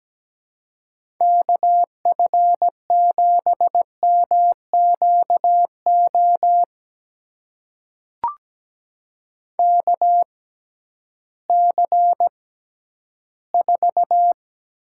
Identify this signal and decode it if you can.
Morse code, heard as a keyed tone, about 17 wpm, spaced as for 11 wpm, 700 Hz. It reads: KF7MQO KC4
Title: USA Call Signs - 6 Characters - 6x Character Spacing - 17wpm